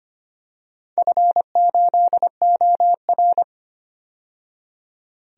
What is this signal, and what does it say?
Morse code, heard as a keyed tone, about 25 wpm, 700 Hz. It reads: F8OR
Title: DXCC-100 Call Signs - 4 Characters - 25wpm